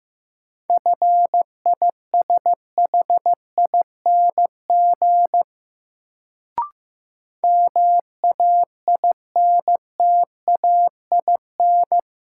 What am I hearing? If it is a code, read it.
Morse code, heard as a keyed tone, about 16 wpm, 700 Hz. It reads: FISHING MAINTAIN
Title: Sentences from Top 1000 Words - Review Part 5 - New Word Drill - 15wpm